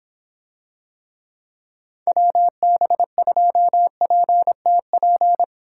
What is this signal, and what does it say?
Morse code, heard as a keyed tone, about 26 wpm, 700 Hz. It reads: WB2PTP